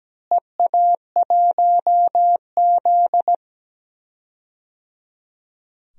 Morse code, heard as a keyed tone, about 17 wpm, 700 Hz. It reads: EA1Z
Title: DXCC-100 Call Signs - 4 Characters - Repeated 3x - 17wpm